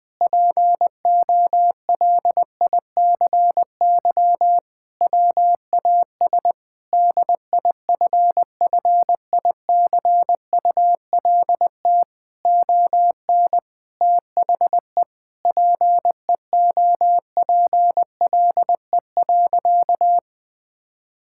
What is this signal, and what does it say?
Morse code, heard as a keyed tone, about 20 wpm, 700 Hz. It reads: POLICY WAS DIFFICULT ON THE PEOPLE.
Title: Sentences from Top 1000 Words - Review Part 5 - 20wpm